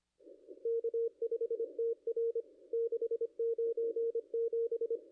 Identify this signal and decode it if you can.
Morse code, heard as a keyed tone, about 25 wpm, 440 Hz. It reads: K5TR 697
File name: contests / ss-cw-k5tr-op-n5rz -0500